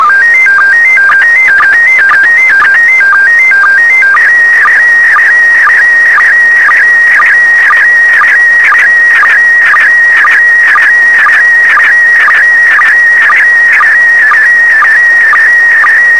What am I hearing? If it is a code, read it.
SSTV (slow-scan television), PD 120.
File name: sstv / PD120-VK4MSL